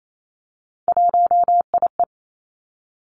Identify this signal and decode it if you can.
Morse code, heard as a keyed tone, about 28 wpm, 700 Hz. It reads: 1IE
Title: Three Challenging Letter-Number Combinations 28wpm